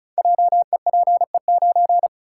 Morse code, heard as a keyed tone, about 36 wpm, 690 Hz. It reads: JEPE9
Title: Code Groups - 5 Letter:Numbers - 35wpm